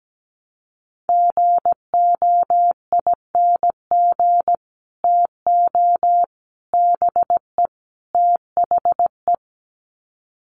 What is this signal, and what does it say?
Morse code, heard as a keyed tone, about 17 wpm, 700 Hz. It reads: GOING TO BE THE
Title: Sets of 4 Words 17wpm